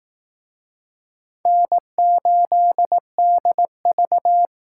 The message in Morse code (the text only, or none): N8DV